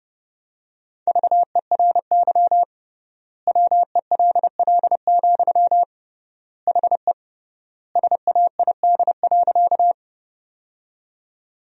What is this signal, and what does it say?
Morse code, heard as a keyed tone, about 30 wpm, 700 Hz. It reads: VERY WELL, HE SAID.
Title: 3x Word-Spacing - Sentences from Top 200 Words - 30wpm